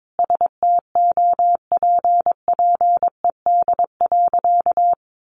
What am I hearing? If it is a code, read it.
Morse code, heard as a keyed tone, about 22 wpm, 700 Hz. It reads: STOPPED.
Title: A Sound Of Thunder 22wpm